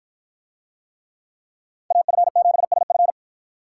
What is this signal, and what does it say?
Morse code, heard as a keyed tone, about 52 wpm, 700 Hz. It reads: AF7SF